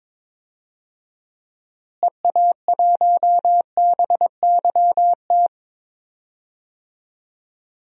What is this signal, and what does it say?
Morse code, heard as a keyed tone, about 22 wpm, 700 Hz. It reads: EA1BYT
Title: DXCC-100 Call Signs - 6 Characters - Repeated 3x - 22wpm